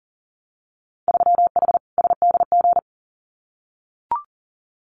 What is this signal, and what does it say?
Morse code, heard as a keyed tone, about 42 wpm, 700 Hz. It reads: 3H SDG
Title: Winter Field Day Exchanges Encore 40wpm